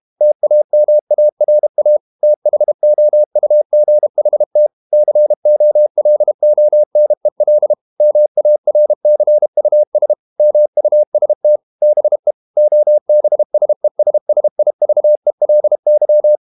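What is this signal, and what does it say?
Morse code, heard as a keyed tone, about 32 wpm, 600 Hz. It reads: TAMARA THOUGHT COLONEL MARCUS MUST BE OBSESSIVELY
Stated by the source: Morse code practice at 32 wpm from CW med Gnister, reading Ken Follett's novel Never in English.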